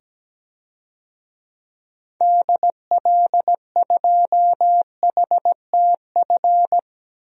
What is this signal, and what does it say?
Morse code, heard as a keyed tone, about 17 wpm, 700 Hz. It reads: DL2HTF
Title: DXCC-100 Call Signs - 6 Characters - 17wpm